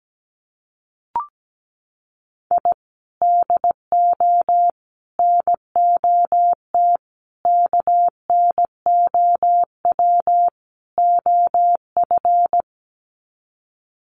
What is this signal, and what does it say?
Morse code, heard as a keyed tone, about 17 wpm, 700 Hz. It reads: I DO NOT KNOW OF
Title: Sets of 5 Words Encore2 17wpm